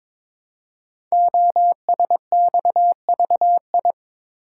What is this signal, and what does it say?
Morse code, heard as a keyed tone, about 22 wpm, 700 Hz. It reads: OSXVI
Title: Code Groups - 5 Letters - 22wpm